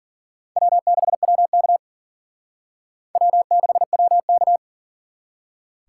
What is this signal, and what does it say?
Morse code, heard as a keyed tone, about 51 wpm, 700 Hz. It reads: W6WK W6WK